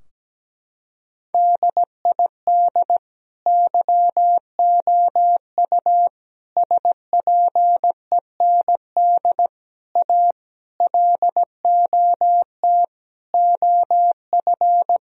Morse code, heard as a keyed tone, about 17 wpm, 700 Hz. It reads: DID YOU SPEND A LOT OF
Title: Sentences from Top 1000 Words - Review Part 4 - 17wpm